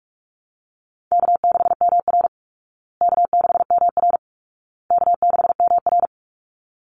Morse code, heard as a keyed tone, about 44 wpm, 700 Hz. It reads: K6MR K6MR K6MR
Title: Easy Call Signs Repeated 3x - 45wpm